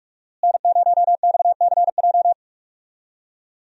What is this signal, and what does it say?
Morse code, heard as a keyed tone, about 42 wpm, 700 Hz. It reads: N0XCJ
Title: USA Call Signs - 5 Characters - Repeated 3x - 45wpm